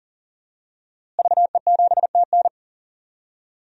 Morse code, heard as a keyed tone, about 37 wpm, 700 Hz. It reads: VE7TN
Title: DXCC-100 Call Signs - 5 Characters - Repeated 3x - 40wpm